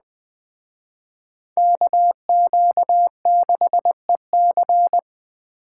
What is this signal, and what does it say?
Morse code, heard as a keyed tone, about 21 wpm, 700 Hz. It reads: KQ6EC